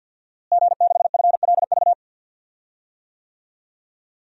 Morse code, heard as a keyed tone, about 50 wpm, 700 Hz. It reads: G6FLV
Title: DXCC-100 Call Signs - 5 Characters - Repeated 3x - 50wpm